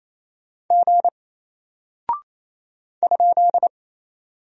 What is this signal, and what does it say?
Morse code, heard as a keyed tone, about 28 wpm, 700 Hz. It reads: G ?
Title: Single Character (Letter:Number:Punctuation:Prosign) Alt 28wpm